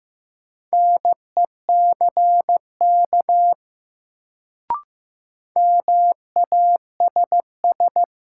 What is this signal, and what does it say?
Morse code, heard as a keyed tone, about 15 wpm, 700 Hz. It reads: NECK MASS
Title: Sets of 5 Words Encore - New Word Drill - 15wpm